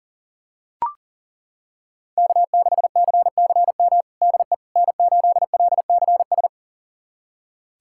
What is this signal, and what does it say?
Morse code, heard as a keyed tone, about 40 wpm, 700 Hz. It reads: K6CCM DE N8LCS
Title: USA Callsign DE Callsign - 5 Characters - 40wpm